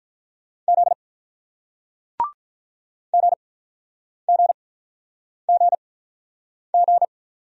Morse code, heard as a keyed tone, about 47 wpm, spaced as for 9 wpm, 700 Hz. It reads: Z GGGG